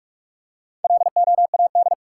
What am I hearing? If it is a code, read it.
Morse code, heard as a keyed tone, about 47 wpm, 700 Hz. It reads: LOAD
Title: Four Letter Words - 45wpm